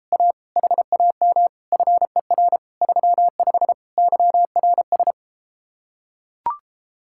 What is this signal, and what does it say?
Morse code, heard as a keyed tone, about 33 wpm, 700 Hz. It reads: A HAM FER 35 YRS